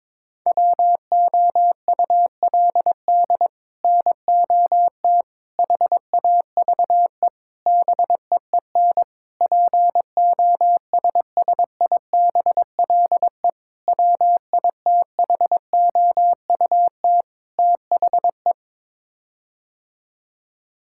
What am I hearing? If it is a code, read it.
Morse code, heard as a keyed tone, about 22 wpm, 700 Hz. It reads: WOULD NOT HAVE BEEN POSSIBLE WITHOUT THE
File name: practice / 22wpm / Sets of 7 Words - 22wpm